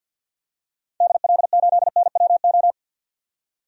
Morse code, heard as a keyed tone, about 46 wpm, 700 Hz. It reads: DL8NWO
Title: DXCC-100 Call Signs - 6 Characters - 50wpm